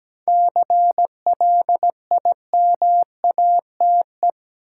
Morse code, heard as a keyed tone, about 17 wpm, 700 Hz. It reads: CLIMATE
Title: Sentences from Top 1600 Words - New Word Drill Alt - 17wpm